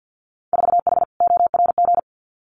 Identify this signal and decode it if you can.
Morse code, heard as a keyed tone, about 50 wpm, 700 Hz. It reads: 4H ORG
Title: Winter Field Day Exchanges Encore 50wpm